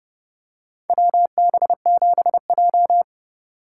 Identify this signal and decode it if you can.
Morse code, heard as a keyed tone, about 31 wpm, 700 Hz. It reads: WB7J
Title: USA Call Signs - 4 Characters - 3x Character Spacing - 30wpm